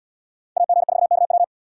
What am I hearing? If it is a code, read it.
Morse code, heard as a keyed tone, about 73 wpm, 700 Hz. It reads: AG4KW